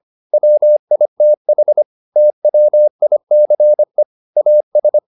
Morse code, heard as a keyed tone, about 25 wpm, 600 Hz. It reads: WITH TWICE AS